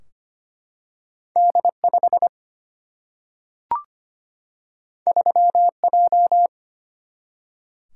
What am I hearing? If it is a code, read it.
Morse code, heard as a keyed tone, about 25 wpm, 700 Hz. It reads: D 5 3 J
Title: Single-Letter Single-Number 25wpm